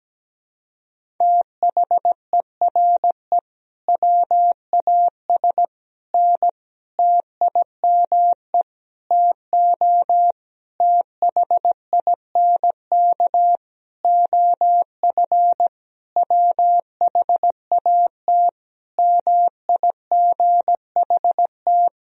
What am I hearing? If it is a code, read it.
Morse code, heard as a keyed tone, about 17 wpm, 700 Hz. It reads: THERE WAS N TIME TO THINK OF WHAT MIGHT
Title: Sentences from Top 400 Words and Abbreviations 17wpm